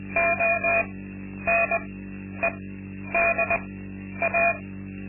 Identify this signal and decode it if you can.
Morse code, heard as a keyed tone, about 20 wpm, spaced as for 10 wpm, 680 Hz, with noise. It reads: ONEDA
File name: Linksmakalnis-2